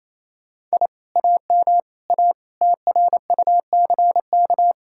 Morse code, heard as a keyed tone, about 35 wpm, 700 Hz. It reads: I AM A TRUCK